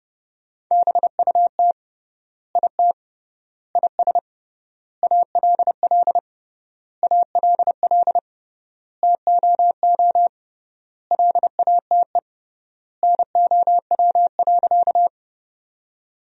Morse code, heard as a keyed tone, about 30 wpm, 700 Hz. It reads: BUT IT IS ALL ALL TOO LATE NOW.
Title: 3x Word-Spacing - Sentences from Top 300 Words - 30wpm